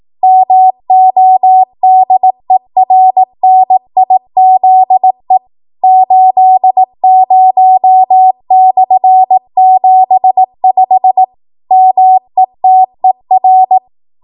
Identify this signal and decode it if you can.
Morse code, heard as a keyed tone, about 18 wpm, 750 Hz. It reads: MODERNIZE 80/75 METER
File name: Morse / 18WPM